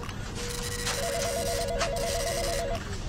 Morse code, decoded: none